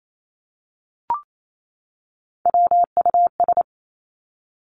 Morse code, heard as a keyed tone, about 28 wpm, 700 Hz. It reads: WUS
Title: Three Challenging Letter-Number Combinations 28wpm